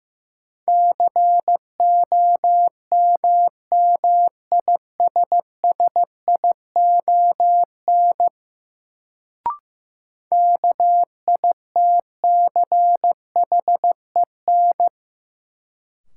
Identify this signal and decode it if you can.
Morse code, heard as a keyed tone, about 15 wpm, 700 Hz. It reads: COMMISSION KITCHEN
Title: Sentences from Top 1400 Words - New Word Drill Alt - 15wpm